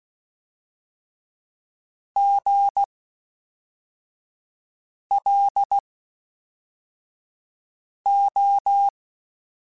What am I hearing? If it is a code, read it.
Morse code, heard as a keyed tone, about 16 wpm, 800 Hz. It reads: G L O
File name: Special INGs / Special INGs-mp3s / Fast ing @10